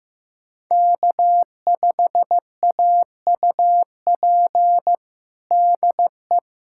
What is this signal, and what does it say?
Morse code, heard as a keyed tone, about 15 wpm, 700 Hz. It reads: K5AUP DE
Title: USA Callsign DE Callsign 15wpm